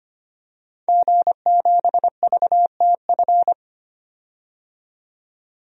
Morse code, heard as a keyed tone, about 25 wpm, 700 Hz. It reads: G7VTF
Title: DXCC-100 Call Signs - 5 Characters - 25wpm